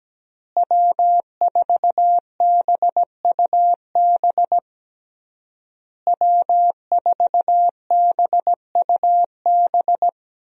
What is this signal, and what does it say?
Morse code, heard as a keyed tone, about 17 wpm, 700 Hz. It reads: W4BUB W4BUB
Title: USA Call Signs - 5 Characters - Repeated 3x - 17wpm